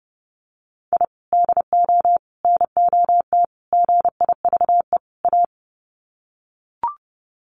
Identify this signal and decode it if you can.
Morse code, heard as a keyed tone, about 29 wpm, 700 Hz. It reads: I DO NOT GIVE A
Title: Sets of 5 Words Encore2 30wpm